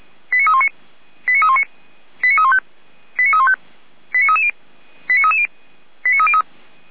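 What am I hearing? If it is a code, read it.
Five-tone selective call (ZVEI), heard as quick runs of five tones, tones of about 70 ms. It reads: 89319 89319 89315 89315 89330 89330 89303
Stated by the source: ZVEI five-tone selective call received by SDR.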